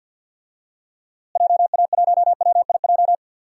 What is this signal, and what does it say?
Morse code, heard as a keyed tone, about 50 wpm, 700 Hz. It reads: JA1WIJ
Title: DXCC-100 Call Signs - 6 Characters - Repeated 2x - 50wpm